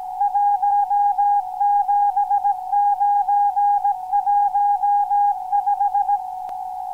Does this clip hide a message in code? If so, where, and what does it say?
Yes: Morse code from 0.2 to 6.2 s, heard as a keyed tone, about 17 wpm, 820 Hz. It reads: 17915